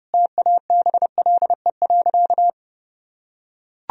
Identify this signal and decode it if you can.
Morse code, heard as a keyed tone, about 30 wpm, 700 Hz. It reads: TABLE.
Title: Sentences from Top 1000 Words - Review Part 4 - 30wpm